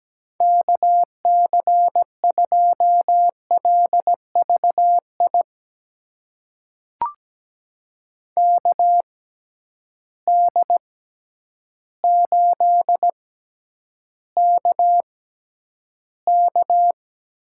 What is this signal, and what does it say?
Morse code, heard as a keyed tone, about 17 wpm, spaced as for 9 wpm, 700 Hz. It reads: KC2LVI KD8KK
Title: USA Call Signs - 6x Character Spacing - 17wpm